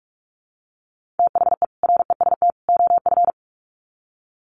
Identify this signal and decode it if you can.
Morse code, heard as a keyed tone, about 44 wpm, 700 Hz. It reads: THE REST OF
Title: Sets of 3 Words 45wpm